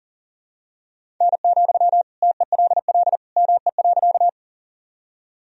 Morse code, heard as a keyed tone, about 40 wpm, 700 Hz. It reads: N, TELL ME.
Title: Sentences from Top 200 Words and Abbreviations 40wpm